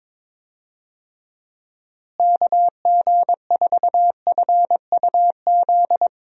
Morse code, heard as a keyed tone, about 22 wpm, 700 Hz. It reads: KG4FUZ
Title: USA Call Signs - 6 Characters - Repeated 2x - 22wpm